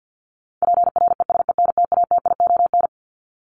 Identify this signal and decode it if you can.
Morse code, heard as a keyed tone, about 50 wpm, 700 Hz. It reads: PRESENTATION
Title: Top 1501-1750 Words 50wpm